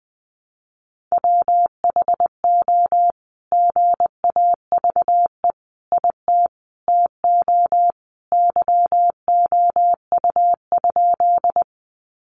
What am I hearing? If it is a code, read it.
Morse code, heard as a keyed tone, about 20 wpm, 700 Hz. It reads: WHO GAVE IT TO YOU?
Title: Sentences from Top 500 Words 20wpm